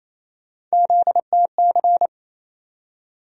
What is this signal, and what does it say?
Morse code, heard as a keyed tone, about 28 wpm, 700 Hz. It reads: ZTC
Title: USA Call Sign Postfixes - 3 Letters 28wpm